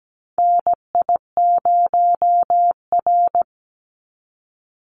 Morse code, heard as a keyed tone, about 17 wpm, 700 Hz. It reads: NI0R
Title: Call Signs - 17wpm